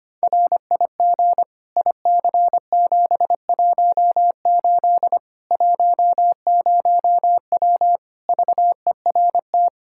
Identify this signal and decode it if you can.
Morse code, heard as a keyed tone, about 25 wpm, 700 Hz. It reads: RIG IC718 10W VERT
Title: Practice QSO - Exchange 2 - Info - 25wpm